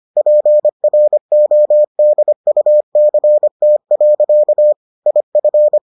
Morse code, heard as a keyed tone, about 25 wpm, 600 Hz. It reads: PRODUCT. IF